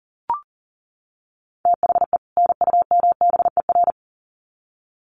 Morse code, heard as a keyed tone, about 40 wpm, 700 Hz. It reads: THE NUMBER
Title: Sets of 2 Words 40wpm